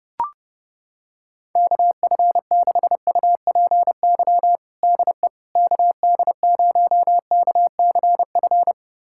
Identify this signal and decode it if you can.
Morse code, heard as a keyed tone, about 30 wpm, 700 Hz. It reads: KF6UPY DE KD0KCF
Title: USA Callsign DE Callsign - 6 Characters - 30wpm